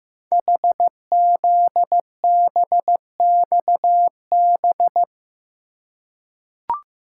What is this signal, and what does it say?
Morse code, heard as a keyed tone, about 15 wpm, 690 Hz. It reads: HZBXB
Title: Code Groups - 5 Letters - 15wpm